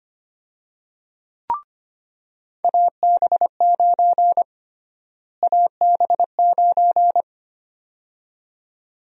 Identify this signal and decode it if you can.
Morse code, heard as a keyed tone, about 25 wpm, 700 Hz. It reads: AB9 AB9
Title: USA Call Sign Prefixes - Repeated 2x - 25wpm